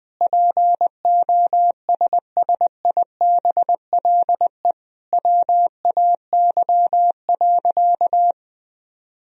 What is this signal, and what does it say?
Morse code, heard as a keyed tone, about 20 wpm, 700 Hz. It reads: POSSIBLE WAY.
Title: Sentences from Top 500 Words and Abbreviations 20wpm